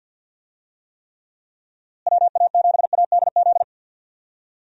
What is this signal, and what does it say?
Morse code, heard as a keyed tone, about 48 wpm, 700 Hz. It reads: WA7ADZ
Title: USA Call Signs - 6 Characters - Repeated 2x - 50wpm